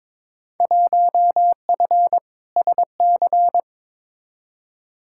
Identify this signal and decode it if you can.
Morse code, heard as a keyed tone, about 21 wpm, 700 Hz. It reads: 1F SC
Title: Field Day Exchanges Encore - 22wpm